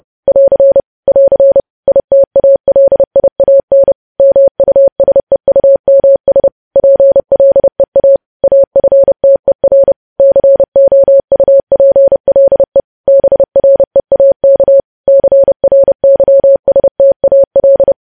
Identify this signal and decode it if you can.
Morse code, heard as a keyed tone, about 30 wpm, 550 Hz. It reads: <AR> <AR> ITALIAN MUSEUMS PLEA AFTER COUPLE BREAK CRYSTAL
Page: News Headlines in Morse Code at 30 WPM